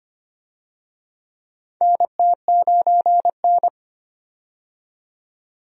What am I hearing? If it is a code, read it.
Morse code, heard as a keyed tone, about 25 wpm, 700 Hz. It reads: NT9N
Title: USA Call Signs - 4 Characters - 25wpm